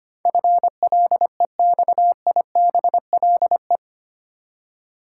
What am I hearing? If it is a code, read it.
Morse code, heard as a keyed tone, about 25 wpm, 700 Hz. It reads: FLEXIBLE
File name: Eight Letter Words - 25wpm